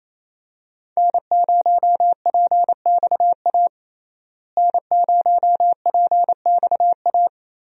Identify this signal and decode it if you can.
Morse code, heard as a keyed tone, about 27 wpm, 700 Hz. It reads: N0PXA N0PXA
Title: USA Call Signs - 5 Characters - Repeated 3x - 28wpm